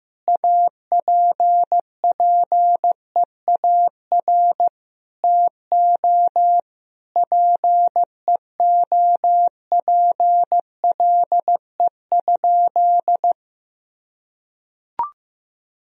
Morse code, heard as a keyed tone, about 16 wpm, 700 Hz. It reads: APPEAR TO PEOPLE?
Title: Sentences from Top 1000 Words - Review Part 2 - 15wpm